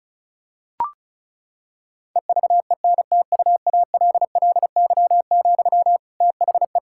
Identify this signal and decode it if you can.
Morse code, heard as a keyed tone, about 36 wpm, 700 Hz. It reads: EVENTUALLY, THE
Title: Sentences from Top 1000 Words - Review Part 6 - 35wpm